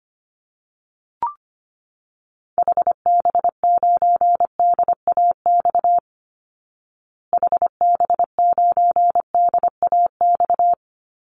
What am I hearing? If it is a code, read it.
Morse code, heard as a keyed tone, about 25 wpm, 700 Hz. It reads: HB9DAX HB9DAX
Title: Call Signs Repeated 2x - 25wpm